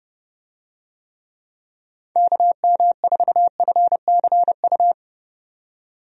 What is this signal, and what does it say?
Morse code, heard as a keyed tone, about 30 wpm, 700 Hz. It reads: KM4FCU
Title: USA Call Signs - 6 Characters - 30wpm